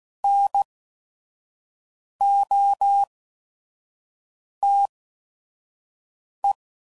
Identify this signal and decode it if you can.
Morse code, heard as a keyed tone, about 16 wpm, spaced as for 5 wpm, 800 Hz. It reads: NOTE